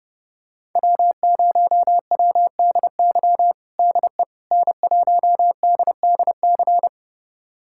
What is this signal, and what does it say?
Morse code, heard as a keyed tone, about 30 wpm, 700 Hz. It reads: W0WDY DE N1DDC
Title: USA Callsign DE Callsign - 5 Characters - 30wpm